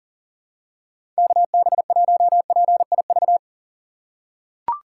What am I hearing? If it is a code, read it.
Morse code, heard as a keyed tone, about 40 wpm, 700 Hz. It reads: KB1PIV
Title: USA Call Signs - 6 Characters - 4x Character Spacing - 40wpm